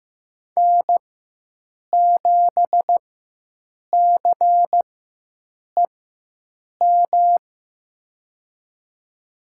Morse code, heard as a keyed tone, about 15 wpm, spaced as for 7 wpm, 700 Hz. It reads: N7CEM
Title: USA Call Signs - 5 Characters - 4x Character Spacing - 15wpm